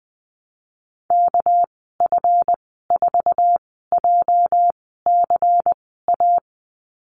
Morse code, heard as a keyed tone, about 20 wpm, spaced as for 14 wpm, 700 Hz. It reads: KF4JCA
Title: 231 Lesson 37g - Letter x - Call Signs 20wpm